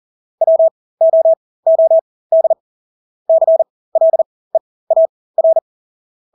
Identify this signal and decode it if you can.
Morse code, heard as a keyed tone, about 40 wpm, spaced as for 20 wpm, 650 Hz. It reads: WOOD CLEAR
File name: Engelske ord 40-20 wpm